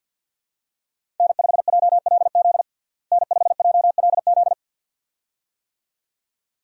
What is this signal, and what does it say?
Morse code, heard as a keyed tone, about 51 wpm, 700 Hz. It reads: N5JLZ N5JLZ
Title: USA Call Signs - 5 Characters - Repeated 2x - 50wpm